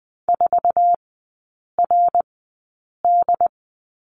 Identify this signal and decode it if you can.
Morse code, heard as a keyed tone, about 20 wpm, spaced as for 8 wpm, 700 Hz. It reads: 4RD
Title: 067 Lesson 12e - Letter d - Three Characters 20wpm